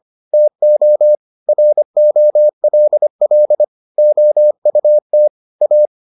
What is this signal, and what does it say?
Morse code, heard as a keyed tone, about 25 wpm, 600 Hz. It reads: TO ROLL OUT A